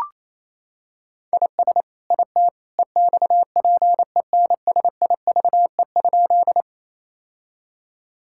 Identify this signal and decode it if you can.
Morse code, heard as a keyed tone, about 27 wpm, 700 Hz. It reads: IS IT EXPENSIVE?
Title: Sentences from Top 1000 Words - Review Part 6 - 28wpm